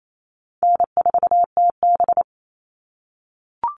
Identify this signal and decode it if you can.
Morse code, heard as a keyed tone, about 28 wpm, 700 Hz. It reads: N4TB
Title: Easy Call Signs - 28wpm